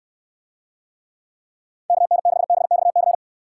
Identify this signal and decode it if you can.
Morse code, heard as a keyed tone, about 62 wpm, 700 Hz. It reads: KN6BXC